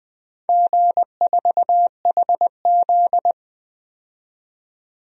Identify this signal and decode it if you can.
Morse code, heard as a keyed tone, about 20 wpm, 700 Hz. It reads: G4HZ
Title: DXCC-100 Call Signs - 4 Characters - 20wpm